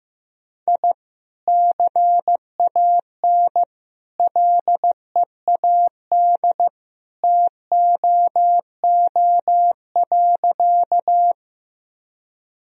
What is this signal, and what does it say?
Morse code, heard as a keyed tone, about 16 wpm, 700 Hz. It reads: I CAN LEAD TOO.
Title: Sentences from Top 1000 Words - Review Part 2 - 15wpm